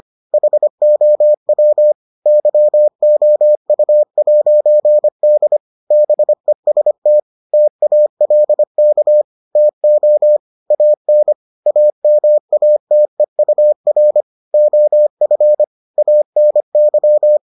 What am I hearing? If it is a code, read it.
Morse code, heard as a keyed tone, about 25 wpm, 600 Hz. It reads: HOW YOU'D BEST TALK TO AN AMATEUR OF ANY